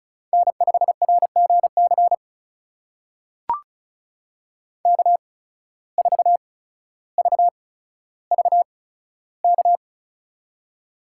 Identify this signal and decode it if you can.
Morse code, heard as a keyed tone, about 36 wpm, spaced as for 15 wpm, 700 Hz. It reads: N5RGC K4VVK
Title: USA Call Signs - 5 Characters - 8x Character Spacing - 35wpm